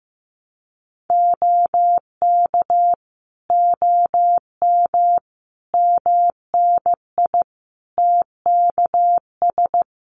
Morse code, heard as a keyed tone, about 15 wpm, 700 Hz. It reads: OK OM MNI TKS
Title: Common QSO Phrases 15wpm